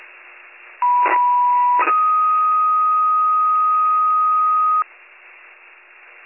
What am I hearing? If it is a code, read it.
Two-tone page, heard as a one-second tone, then a three-second tone, about 980 then 1250 Hz, with noise. It is heard from 0.8 to 4.8 s.